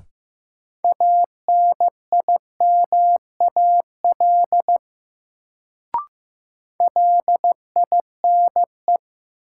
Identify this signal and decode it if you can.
Morse code, heard as a keyed tone, about 15 wpm, 700 Hz. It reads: ANIMAL LINE
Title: Sentences from Top 1000 Words - Review Part 1 - New Word Drill - 15wpm